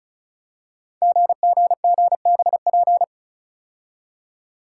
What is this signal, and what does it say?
Morse code, heard as a keyed tone, about 36 wpm, 690 Hz. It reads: GGGBP
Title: Code Groups - 5 Letters - 35wpm